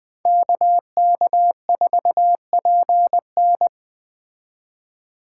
Morse code, heard as a keyed tone, about 20 wpm, 700 Hz. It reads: KK4PN